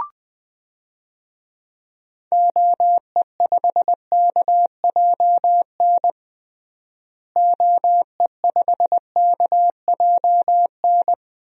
Morse code, heard as a keyed tone, about 21 wpm, 700 Hz. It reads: OE5KJN OE5KJN
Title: DXCC-100 Call Signs - 6 Characters - Repeated 3x - 20wpm